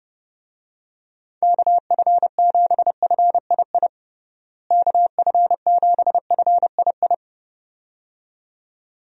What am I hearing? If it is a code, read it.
Morse code, heard as a keyed tone, about 30 wpm, 700 Hz. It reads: KF7FII KF7FII
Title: USA Call Signs - 6 Characters - Repeated 2x - 30wpm